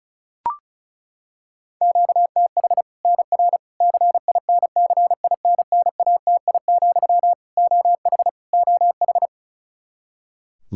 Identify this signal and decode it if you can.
Morse code, heard as a keyed tone, about 36 wpm, 700 Hz. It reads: QTH NR CINCINNATI, OH OH
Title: Practice QSO - Exchange 1 - Beginning - 35wpm